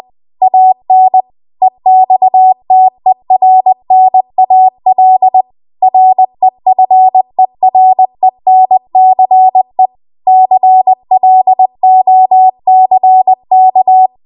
Morse code, heard as a keyed tone, about 20 wpm, 750 Hz. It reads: AN EXTERNAL REFERENCE CLOCK
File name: Archive / 20 WPM / 20WPM